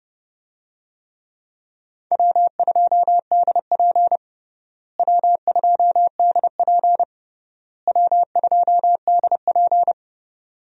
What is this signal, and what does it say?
Morse code, heard as a keyed tone, about 30 wpm, 700 Hz. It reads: W2DP W2DP W2DP
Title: USA Call Signs - 4 Characters - Repeated 3x - 30wpm